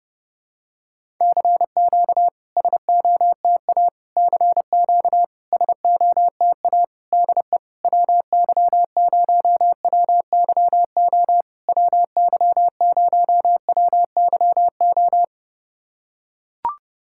Morse code, heard as a keyed tone, about 30 wpm, 700 Hz. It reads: CQ SOTA CQ SOTA DE WY0WYO WY0WYO
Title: SOTA QSO Practice 30wpm